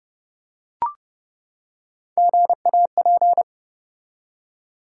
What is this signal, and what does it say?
Morse code, heard as a keyed tone, about 30 wpm, 700 Hz. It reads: GAP
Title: Three Letter Words - 30wpm